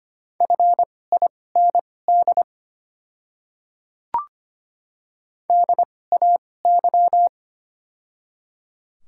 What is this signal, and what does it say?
Morse code, heard as a keyed tone, about 25 wpm, spaced as for 18 wpm, 700 Hz. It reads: FIND DAY
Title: 2x Character Spacing - Top 200 Words - 25wpm